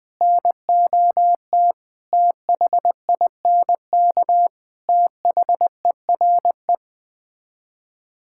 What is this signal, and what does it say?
Morse code, heard as a keyed tone, about 21 wpm, 700 Hz. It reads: NOT THINK THERE
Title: Build A Phrase - 4-Word Phrases - 20wpm